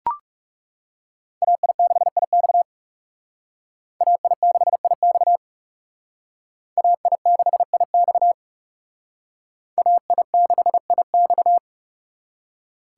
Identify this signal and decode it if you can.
Morse code, heard as a keyed tone, about 37 wpm, 700 Hz. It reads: AI6IX AI6IX AI6IX AI6IX
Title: Speed Racing - USA 2x2 Call Signs 30wpm